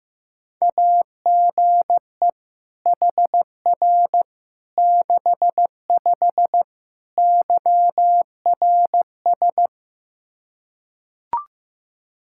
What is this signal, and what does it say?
Morse code, heard as a keyed tone, about 15 wpm, 700 Hz. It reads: AGE HR 65 YRS
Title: Practice QSO - Exchange 3 - More Info - 15wpm